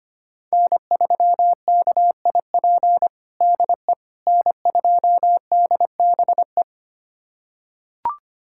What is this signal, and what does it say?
Morse code, heard as a keyed tone, about 25 wpm, 700 Hz. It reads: N3KIP DE N2DBE
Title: USA Callsign DE Callsign - 5 Characters - 25wpm